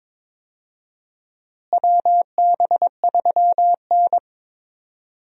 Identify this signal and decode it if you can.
Morse code, heard as a keyed tone, about 22 wpm, 700 Hz. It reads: WB3N
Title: USA Call Signs - 4 Characters - Repeated 2x - 22wpm